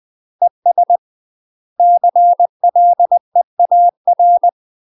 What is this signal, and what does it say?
Morse code, heard as a keyed tone, about 20 wpm, 700 Hz. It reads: ES CLEAR
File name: UK MASTER QSOs TRAINING/20WPM